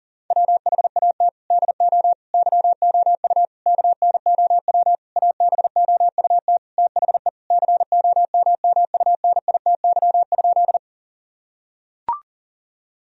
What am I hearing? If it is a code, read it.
Morse code, heard as a keyed tone, about 40 wpm, 690 Hz. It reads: WHAT DO YOU KNOW ABOUT THE COMMUNITY?
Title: Sentences from Top 1000 Words - Review Part 4 - 40wpm